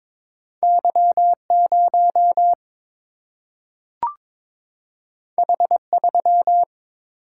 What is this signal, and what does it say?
Morse code, heard as a keyed tone, about 22 wpm, 700 Hz. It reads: Y0 H3
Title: Single-Letter Single-Number 22wpm